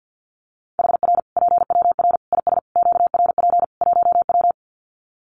Morse code, heard as a keyed tone, about 50 wpm, 700 Hz. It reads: HR PWR IS QRP 1W
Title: Common QSO Phrases 50wpm